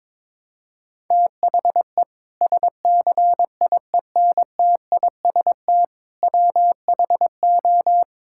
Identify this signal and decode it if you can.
Morse code, heard as a keyed tone, about 22 wpm, 700 Hz. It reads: THE SCIENTIST WHO
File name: practice / 22wpm / Sentences with 8-Word Phrases - 22wpm